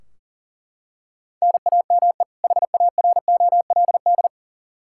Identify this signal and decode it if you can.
Morse code, heard as a keyed tone, about 40 wpm, 700 Hz. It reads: NAME HAROLD
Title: Practice QSO - Exchange 1 - Beginning - 40wpm